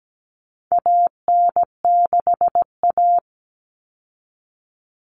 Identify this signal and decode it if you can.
Morse code, heard as a keyed tone, about 17 wpm, 700 Hz. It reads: AN6A
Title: Common Cut Numbers - 4-Digit 17wpm